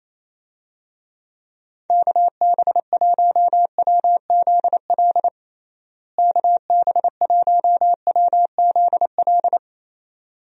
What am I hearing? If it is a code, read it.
Morse code, heard as a keyed tone, about 27 wpm, 700 Hz. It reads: KB1WZL KB1WZL
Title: USA Call Signs - 6 Characters - Repeated 2x - 28wpm